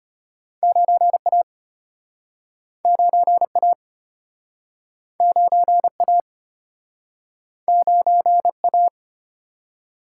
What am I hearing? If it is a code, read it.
Morse code, heard as a keyed tone, about 35 wpm, 700 Hz. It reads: 9A 9A 9A 9A